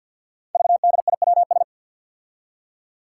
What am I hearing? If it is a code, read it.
Morse code, heard as a keyed tone, about 47 wpm, 700 Hz. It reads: VDIWS
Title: Five Challenging Letter-Number Combinations 50wpm